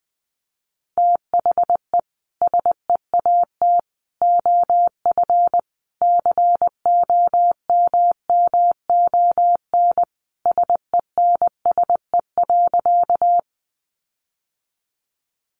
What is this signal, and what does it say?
Morse code, heard as a keyed tone, about 20 wpm, 700 Hz. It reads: THE SEAT OF COMMON SENSE.
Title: Sentences from Top 1500 Words 20wpm